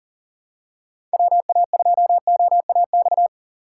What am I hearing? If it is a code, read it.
Morse code, heard as a keyed tone, about 40 wpm, 700 Hz. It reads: WA2OAX
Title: USA Call Signs - 6 Characters - 6x Character Spacing - 40wpm